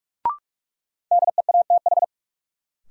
Morse code, heard as a keyed tone, about 46 wpm, 700 Hz. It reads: DEATH
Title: Sentences from Top 1100 Words - New Word Drill - 45wpm